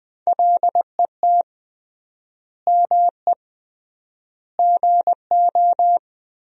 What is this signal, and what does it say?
Morse code, heard as a keyed tone, about 20 wpm, 700 Hz. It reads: LET ME GO
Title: 3x Word-Spacing - Sentences from Top 300 Words - 20wpm